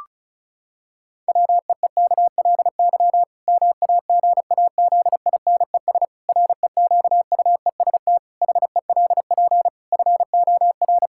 Morse code, heard as a keyed tone, about 34 wpm, 700 Hz. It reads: WEEKLY MAGAZINES REQUEST HELP FOR
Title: Pangram Challenge v2 - 35wpm